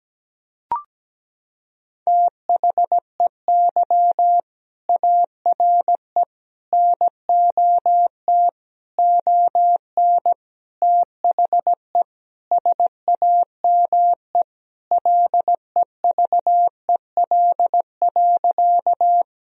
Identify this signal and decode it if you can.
Morse code, heard as a keyed tone, about 17 wpm, 700 Hz. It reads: THEY ARE NOT ON THE SAME LEVEL.
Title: Sentences from Top 1000 Words - Review Part 4 - 17wpm